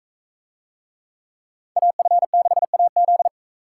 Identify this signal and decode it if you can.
Morse code, heard as a keyed tone, about 42 wpm, 700 Hz. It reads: AF6AZ